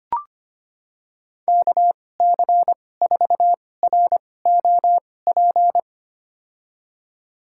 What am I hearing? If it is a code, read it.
Morse code, heard as a keyed tone, about 25 wpm, spaced as for 18 wpm, 700 Hz. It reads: KC4ROP
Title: USA Call Signs - 2x Character Spacing - 25wpm